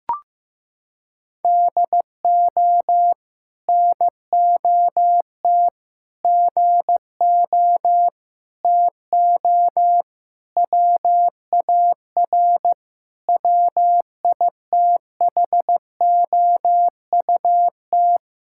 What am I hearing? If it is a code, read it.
Morse code, heard as a keyed tone, about 15 wpm, 700 Hz. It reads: DO NOT GO TO WAR WITHOUT
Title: Sentences from Top 1000 Words - Review Part 5 - 15wpm